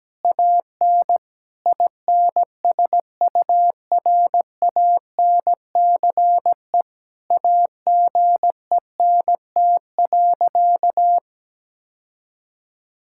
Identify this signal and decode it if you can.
Morse code, heard as a keyed tone, about 17 wpm, 700 Hz. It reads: AN INSURANCE AGENT.
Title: Sentences from Top 1600 Words 17wpm